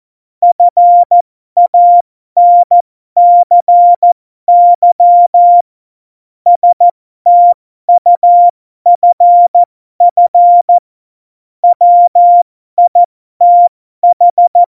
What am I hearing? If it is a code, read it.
Morse code, heard as a keyed tone, about 14 wpm, spaced as for 12 wpm, 700 Hz. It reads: FANCY STUFF WITH
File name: Morsekurs / Witze